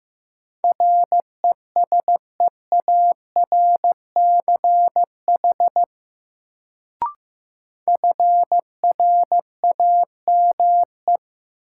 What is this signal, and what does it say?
Morse code, heard as a keyed tone, about 15 wpm, 700 Hz. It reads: RESEARCH FRAME
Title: Sentences from Top 1400 Words - New Word Drill Alt - 15wpm